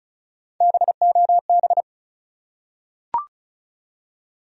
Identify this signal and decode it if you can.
Morse code, heard as a keyed tone, about 34 wpm, 700 Hz. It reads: BOB
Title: USA Common Names - Max 4 Letters - 35wpm